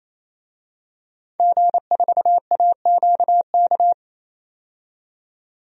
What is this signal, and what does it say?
Morse code, heard as a keyed tone, about 28 wpm, 700 Hz. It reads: G4AQK
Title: DXCC-100 Call Signs - 5 Characters - 28wpm